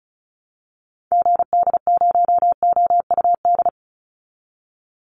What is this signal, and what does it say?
Morse code, heard as a keyed tone, about 34 wpm, 700 Hz. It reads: GD0OUD
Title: Call Signs Repeated 3x - 35wpm